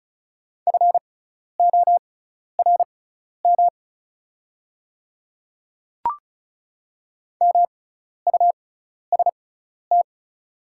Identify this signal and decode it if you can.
Morse code, heard as a keyed tone, about 36 wpm, spaced as for 12 wpm, 700 Hz. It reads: FORM MUST
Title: 6x Character Spacing - Top 200 Words - 35wpm